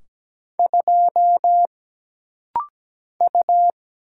Morse code, heard as a keyed tone, about 17 wpm, 700 Hz. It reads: 2 U
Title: Single Letter-Number Alt - Rapid-Fire 17wpm